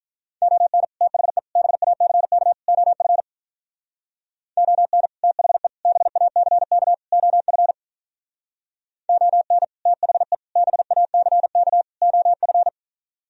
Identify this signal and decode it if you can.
Morse code, heard as a keyed tone, about 43 wpm, 700 Hz. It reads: ON THE BACK OF ON THE BACK OF ON THE BACK OF
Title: Speed-Racing - Sets of 4 Words 35wpm